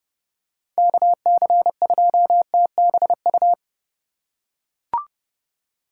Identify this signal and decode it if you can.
Morse code, heard as a keyed tone, about 30 wpm, 700 Hz. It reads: KC2TBU
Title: USA Call Signs - 6 Characters - 2x Character Spacing - 30wpm